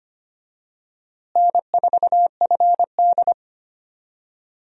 Morse code, heard as a keyed tone, about 25 wpm, 700 Hz. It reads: N4FD
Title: USA Call Signs - 4 Characters - Repeated 2x - 25wpm